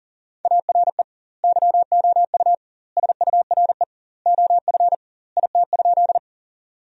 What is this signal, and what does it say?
Morse code, heard as a keyed tone, about 40 wpm, 700 Hz. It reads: ARE YOU SURE OF IT?
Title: 2x Word-Spacing - Sentences from Top 300 Words - 40wpm